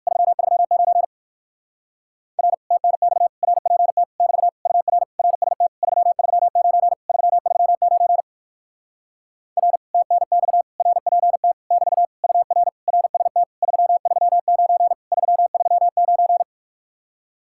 Morse code, heard as a keyed tone, about 56 wpm, 700 Hz. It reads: 339 R TNX RPT <BT> UR RST 339 339 R TNX RPT <BT> UR RST 339 339
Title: Speed-Racing Common QSO Phrases - 45wpm